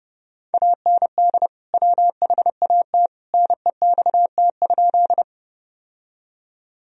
Morse code, heard as a keyed tone, about 32 wpm, 700 Hz. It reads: AND WHAT NEXT?
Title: Sentences from Top 1000 Words - Review Part 1 - 30wpm